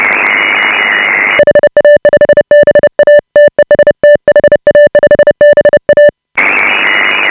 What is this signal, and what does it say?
Morse code, heard as a keyed tone, about 30 wpm, 590 Hz. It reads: HA5BA TEST HA5BA
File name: AUDIO2018 / HA5BA CW